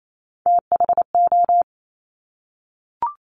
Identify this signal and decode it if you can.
Morse code, heard as a keyed tone, about 28 wpm, 700 Hz. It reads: THO
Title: Top Three Letter Combinations in Words 28wpm